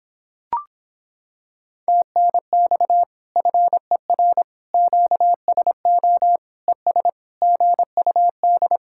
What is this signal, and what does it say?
Morse code, heard as a keyed tone, about 26 wpm, 700 Hz. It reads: TNX FER QSO ES GUD